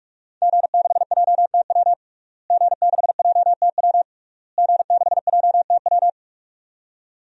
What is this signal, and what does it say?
Morse code, heard as a keyed tone, about 47 wpm, 700 Hz. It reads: G6JTW G6JTW G6JTW
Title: DXCC-100 Call Signs - 5 Characters - Repeated 3x - 45wpm